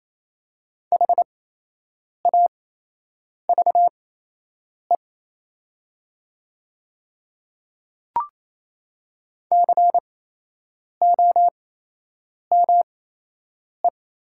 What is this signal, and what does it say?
Morse code, heard as a keyed tone, about 28 wpm, spaced as for 8 wpm, 700 Hz. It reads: HAVE COME
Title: 8x Character Spacing - Top 200 Words - 28wpm